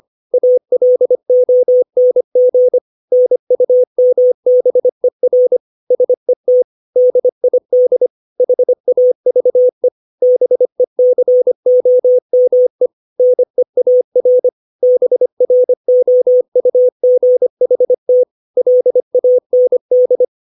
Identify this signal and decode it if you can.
Morse code, heard as a keyed tone, about 25 wpm, 500 Hz. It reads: ALONG NUMBER SET DID HAVE BECOME NEAR BROUGHT LAND
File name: 632 words / 632 Words 25WPM